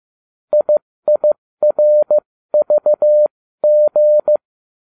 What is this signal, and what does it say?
Morse code, heard as a keyed tone, about 15 wpm, spaced as for 13 wpm, 600 Hz. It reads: IIRVG